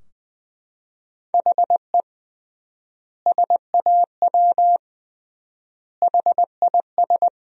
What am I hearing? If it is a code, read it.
Morse code, heard as a keyed tone, about 20 wpm, 700 Hz. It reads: HE SAW HIS
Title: 3x Word-Spacing - Sentences from Top 300 Words - 20wpm